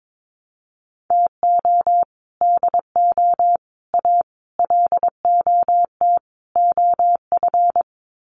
Morse code, heard as a keyed tone, about 22 wpm, 700 Hz. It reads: TO DO A LOT OF
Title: Sets of 5 Words Encore 22wpm